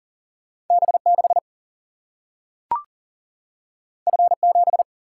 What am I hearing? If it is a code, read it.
Morse code, heard as a keyed tone, about 40 wpm, 700 Hz. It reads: B6 F7
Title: Single-Letter Single-Number 40wpm